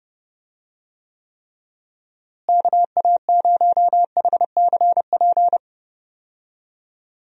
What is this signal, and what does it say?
Morse code, heard as a keyed tone, about 30 wpm, 700 Hz. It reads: KA0HCP
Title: USA Call Signs - 6 Characters - Repeated 3x - 30wpm